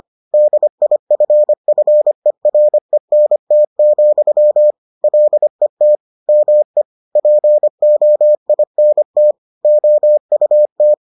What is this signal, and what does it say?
Morse code, heard as a keyed tone, about 25 wpm, 600 Hz. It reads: DIFFERENT, LET ME POINT OUT